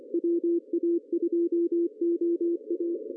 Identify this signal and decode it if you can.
Morse code, heard as a keyed tone, about 26 wpm, 340 Hz. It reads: WA2OA
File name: CW / cqww-cw-k5tr